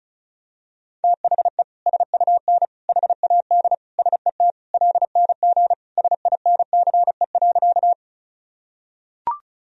Morse code, heard as a keyed tone, about 34 wpm, 700 Hz. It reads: THE SUN HAD SET LNG SINCE.
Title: Sentences from Top 300 Words and Abbreviations 35wpm